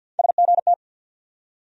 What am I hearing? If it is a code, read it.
Morse code, heard as a keyed tone, about 48 wpm, 700 Hz. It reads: SGT
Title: USA Call Sign Postfixes - 3 Letters 50wpm